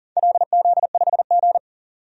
Morse code, heard as a keyed tone, about 40 wpm, 700 Hz. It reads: LZ5G
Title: DXCC-100 Call Signs - 4 Characters - 40wpm